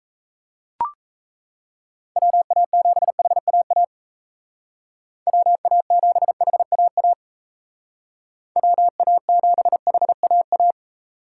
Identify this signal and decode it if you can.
Morse code, heard as a keyed tone, about 37 wpm, 700 Hz. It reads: WA7HAA WA7HAA WA7HAA